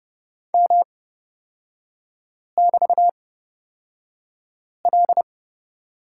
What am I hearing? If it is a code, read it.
Morse code, heard as a keyed tone, about 30 wpm, spaced as for 5 wpm, 700 Hz. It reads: M<BT>L
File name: Lesson 37e - Letter x - Three Characters 30wpm